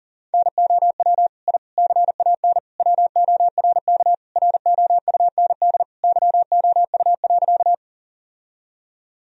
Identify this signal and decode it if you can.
Morse code, heard as a keyed tone, about 40 wpm, 700 Hz. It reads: NOW I CAN WORK ROUND YOU.
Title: Sentences from Top 1000 Words - Review Part 1 - 40wpm